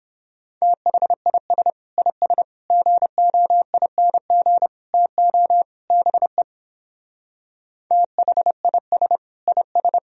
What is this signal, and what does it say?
Morse code, heard as a keyed tone, about 29 wpm, 700 Hz. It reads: THIS IS GOING TO BE THIS IS